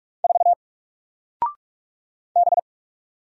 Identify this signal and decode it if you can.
Morse code, heard as a keyed tone, about 44 wpm, 700 Hz. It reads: 4 B
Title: Single Letter-Number Alt 45wpm